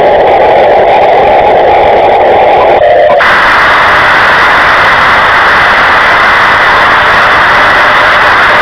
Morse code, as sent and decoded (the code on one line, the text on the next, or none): none